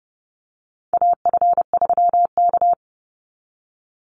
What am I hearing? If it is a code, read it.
Morse code, heard as a keyed tone, about 30 wpm, 700 Hz. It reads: AF3K
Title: Call Signs Repeated 3x - 30wpm